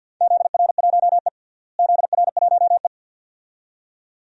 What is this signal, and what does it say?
Morse code, heard as a keyed tone, about 52 wpm, 700 Hz. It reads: ZR1E ZR1E
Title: DXCC-100 Call Signs - 4 Characters - Repeated 2x - 50wpm